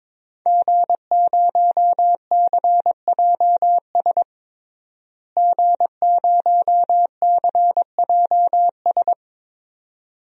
Morse code, heard as a keyed tone, about 21 wpm, 700 Hz. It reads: G0CJS G0CJS
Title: DXCC-100 Call Signs - 5 Characters - Repeated 2x - 22wpm